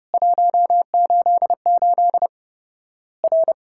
Morse code, heard as a keyed tone, about 30 wpm, 700 Hz. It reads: 188 R
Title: Worked All Europe DX Contest - QTC Exchange - 30wpm